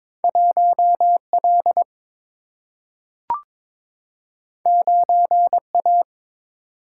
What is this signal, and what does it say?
Morse code, heard as a keyed tone, about 22 wpm, 700 Hz. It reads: 1 L 9 A
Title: Single-Letter Single-Number 22wpm